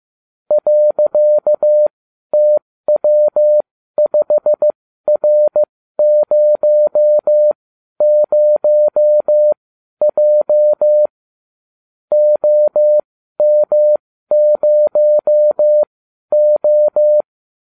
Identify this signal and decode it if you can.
Morse code, heard as a keyed tone, about 15 wpm, spaced as for 12 wpm, 600 Hz. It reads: .TW5R00J OM0O